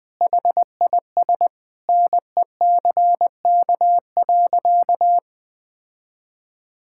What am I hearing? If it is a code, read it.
Morse code, heard as a keyed tone, about 20 wpm, 700 Hz. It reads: HIS NECK.
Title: Sentences with 4-Word Phrases - Spoken Beginning - 20wpm